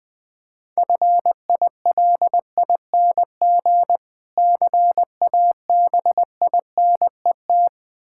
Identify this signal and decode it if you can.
Morse code, heard as a keyed tone, about 20 wpm, 700 Hz. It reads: FILING CABINET
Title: Difficult Words - Speed-Racing with Hints - 20wpm